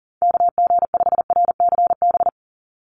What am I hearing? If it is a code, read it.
Morse code, heard as a keyed tone, about 38 wpm, 700 Hz. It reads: KG5RCB
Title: USA Call Signs 40wpm